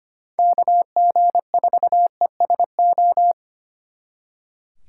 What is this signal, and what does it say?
Morse code, heard as a keyed tone, about 32 wpm, 700 Hz. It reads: KG4ESO